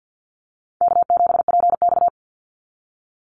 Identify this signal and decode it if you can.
Morse code, heard as a keyed tone, about 50 wpm, 700 Hz. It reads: K7PX
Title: NA SOTA Activator Callsigns 50wpm